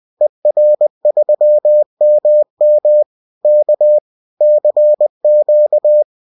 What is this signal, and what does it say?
Morse code, heard as a keyed tone, about 20 wpm, 600 Hz. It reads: ER3MM K CQ